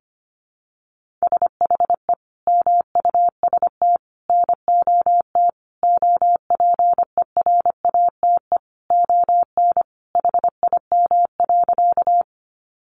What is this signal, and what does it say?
Morse code, heard as a keyed tone, about 25 wpm, 700 Hz. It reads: SHE MUST NOT OPERATE ON HIM.
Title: Sentences from Top 1500 Words 25wpm